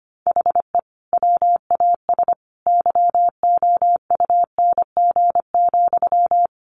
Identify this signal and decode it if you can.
Morse code, heard as a keyed tone, about 25 wpm, 700 Hz. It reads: HE WAS YOUNG,
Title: Sentences from Top 1100 Words 25wpm